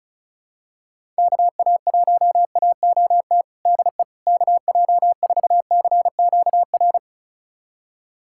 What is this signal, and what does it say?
Morse code, heard as a keyed tone, about 34 wpm, 700 Hz. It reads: KA1AOT DE KJ4CQR
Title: USA Callsign DE Callsign 35wpm